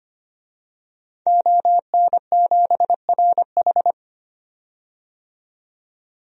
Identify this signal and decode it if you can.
Morse code, heard as a keyed tone, about 25 wpm, 700 Hz. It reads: ON7RH
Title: DXCC-100 Call Signs - 5 Characters - 25wpm